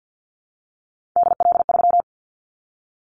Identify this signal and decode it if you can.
Morse code, heard as a keyed tone, about 50 wpm, 700 Hz. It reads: DL3
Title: Callsign Prefixes for Top 50 DXCC - 50wpm